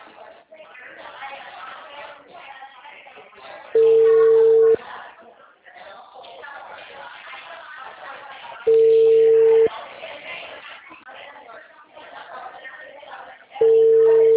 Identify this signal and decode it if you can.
Ringback tone (China, 450 Hz), with speech.